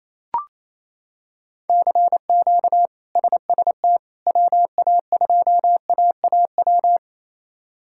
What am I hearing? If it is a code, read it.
Morse code, heard as a keyed tone, about 27 wpm, 700 Hz. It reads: CQ SST WA2AAW
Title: K1USN SST QSO Practice - 28wpm